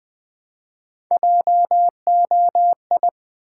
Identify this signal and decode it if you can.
Morse code, heard as a keyed tone, about 20 wpm, 700 Hz. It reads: JOI
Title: Build A Word - 1 Letter At A Time - 3 to 5-Letter Words - 20wpm